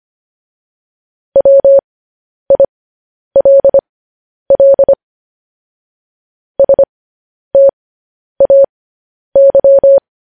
Morse code, heard as a keyed tone, about 25 wpm, spaced as for 10 wpm, 550 Hz. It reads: WILL STAY
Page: news-headlines-in-morse-code-at-10-wpm